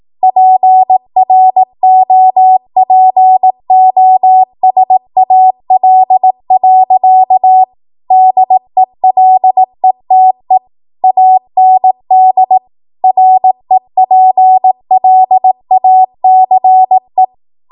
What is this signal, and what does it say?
Morse code, heard as a keyed tone, about 18 wpm, 750 Hz. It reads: PROPOSAL. DELETE AND REPLACE
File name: Morse / 18WPM